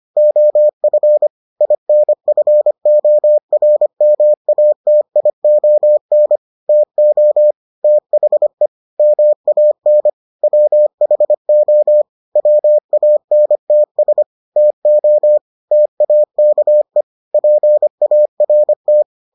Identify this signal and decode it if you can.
Morse code, heard as a keyed tone, about 25 wpm, 600 Hz. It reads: OF INFORMATION TO THE MAN WHO WANTS TO TAKE PART